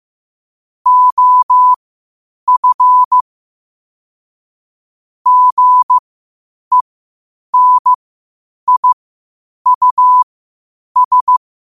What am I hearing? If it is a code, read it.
Morse code, heard as a keyed tone, about 15 wpm, spaced as for 8 wpm, 1000 Hz. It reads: OF GENIUS
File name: qotdfiles / 8wpm